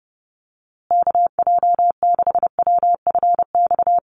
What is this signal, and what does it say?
Morse code, heard as a keyed tone, about 30 wpm, 700 Hz. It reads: KJ6WFX
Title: USA Call Signs Repeated 3x 30wpm